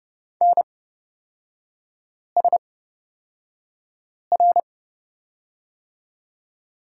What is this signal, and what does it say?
Morse code, heard as a keyed tone, about 31 wpm, spaced as for 5 wpm, 700 Hz. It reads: NSR
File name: Lesson 37e - Letter x - Three Characters 30wpm